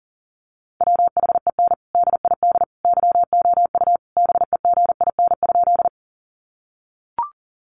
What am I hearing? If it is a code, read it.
Morse code, heard as a keyed tone, about 40 wpm, 700 Hz. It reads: WHEN DID YOU BEGIN?
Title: Sentences from Top 300 Words 40wpm